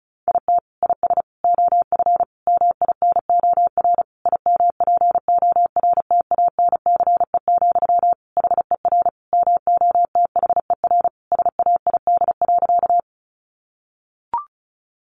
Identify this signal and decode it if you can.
Morse code, heard as a keyed tone, about 34 wpm, 700 Hz. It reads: IT IS OF MINOR IMPORTANCE, HER MOTHER SAID.
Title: Sentences from Top 1500 Words 35wpm